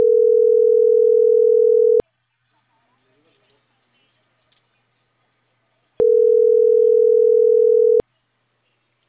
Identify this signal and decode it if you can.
Ringback tone (North America), 440 + 480 Hz.